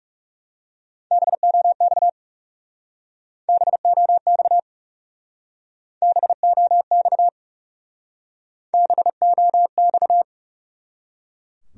Morse code, heard as a keyed tone, about 36 wpm, 700 Hz. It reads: BOX BOX BOX BOX
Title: Speed-Racing Top 500 Words 30wpm